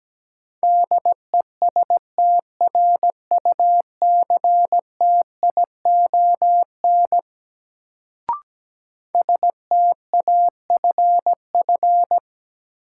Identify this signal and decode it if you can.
Morse code, heard as a keyed tone, about 17 wpm, 700 Hz. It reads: DESTRUCTION STAFF
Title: Sets of 5 Words Encore - New Word Drill - 17wpm